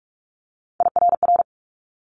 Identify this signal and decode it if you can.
Morse code, heard as a keyed tone, about 46 wpm, 700 Hz. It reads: IRR
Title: Multi-letter Phonemes Quick Fire 45wpm